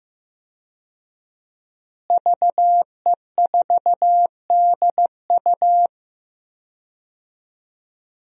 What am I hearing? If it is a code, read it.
Morse code, heard as a keyed tone, about 15 wpm, 700 Hz. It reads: VE4DU
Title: DXCC-100 Call Signs - 5 Characters - Repeated 2x - 15wpm